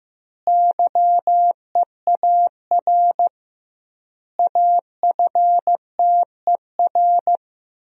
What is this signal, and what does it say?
Morse code, heard as a keyed tone, about 15 wpm, 700 Hz. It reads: YEAR AFTER
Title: 2x Word-Spacing - Sentences from Top 300 Words - 15wpm